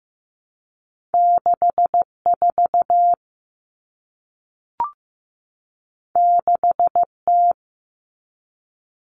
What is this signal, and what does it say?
Morse code, heard as a keyed tone, about 15 wpm, 700 Hz. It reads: 64 6T
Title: Common Cut Numbers - 2-Digit 15wpm